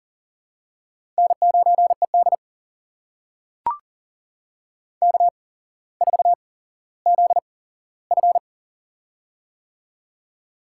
Morse code, heard as a keyed tone, about 40 wpm, spaced as for 17 wpm, 700 Hz. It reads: N9ED K4ZF
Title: USA Call Signs - 4 Characters - 8x Character Spacing - 40wpm